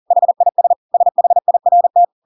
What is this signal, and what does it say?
Morse code, heard as a keyed tone, about 40 wpm, 700 Hz. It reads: HIS SHIRT
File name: Pinocchio / Pinocchio ch8 40 40 ik1hge